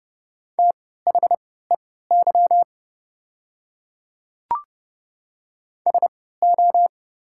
Morse code, heard as a keyed tone, about 30 wpm, spaced as for 17 wpm, 700 Hz. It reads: THEY SO
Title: 3x Character Spacing - Top 100 Words - 30wpm